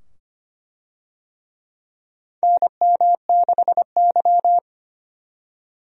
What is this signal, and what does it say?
Morse code, heard as a keyed tone, about 25 wpm, 700 Hz. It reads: NM6Y
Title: USA Call Signs - 4 Characters - 25wpm